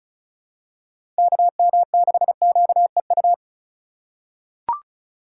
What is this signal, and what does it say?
Morse code, heard as a keyed tone, about 34 wpm, 700 Hz. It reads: KM6QEU
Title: USA Call Signs - 6 Characters - 2x Character Spacing - 35wpm